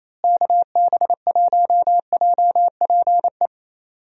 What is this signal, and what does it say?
Morse code, heard as a keyed tone, about 27 wpm, 700 Hz. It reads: KB1JPE
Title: USA Call Signs - 6 Characters - Repeated 3x - 28wpm